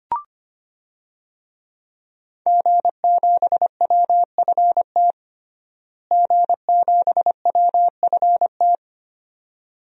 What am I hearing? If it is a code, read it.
Morse code, heard as a keyed tone, about 25 wpm, 700 Hz. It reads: G7WFT G7WFT
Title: DXCC-100 Call Signs - 5 Characters - Repeated 2x - 25wpm